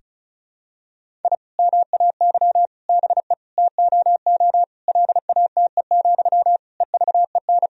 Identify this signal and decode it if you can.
Morse code, heard as a keyed tone, about 34 wpm, 700 Hz. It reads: I MAY BE TOO LATE, EVEN
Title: Sentences from Top 1000 Words - Review Part 1 - 35wpm